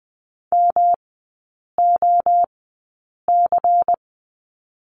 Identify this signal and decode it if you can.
Morse code, heard as a keyed tone, about 20 wpm, spaced as for 8 wpm, 690 Hz. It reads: MOC
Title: 145 Lesson 24e - Letter y - Three Characters 20wpm